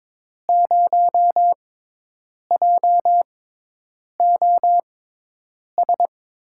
Morse code, heard as a keyed tone, about 22 wpm, spaced as for 8 wpm, 700 Hz. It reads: 0JOS
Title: USA Call Signs - 6 Characters - 6x Character Spacing - 22wpm